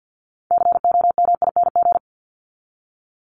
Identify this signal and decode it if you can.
Morse code, heard as a keyed tone, about 50 wpm, 700 Hz. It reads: COMING
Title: ING Suffix - Top 100 Words with Base Words 50wpm